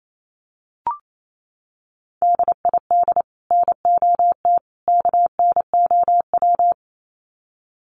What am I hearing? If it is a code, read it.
Morse code, heard as a keyed tone, about 28 wpm, 700 Hz. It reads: DID NOT KNOW
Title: Sets of 3 Words 28wpm